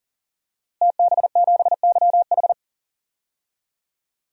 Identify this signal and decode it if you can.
Morse code, heard as a keyed tone, about 40 wpm, 700 Hz. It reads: TB7YH
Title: DXCC-100 Call Signs - 5 Characters - Repeated 3x - 40wpm